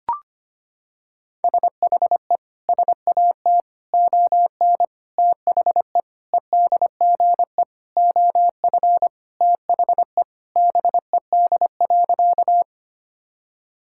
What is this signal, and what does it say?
Morse code, heard as a keyed tone, about 25 wpm, 700 Hz. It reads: SHE SAT ON THE EDGE OF THE BED.
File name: practice / 25wpm / Sentences with 4-Word Phrases - 25wpm